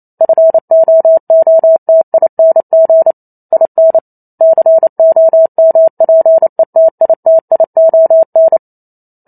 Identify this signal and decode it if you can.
Morse code, heard as a keyed tone, about 28 wpm, 650 Hz. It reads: FOOTING IN COMPETITION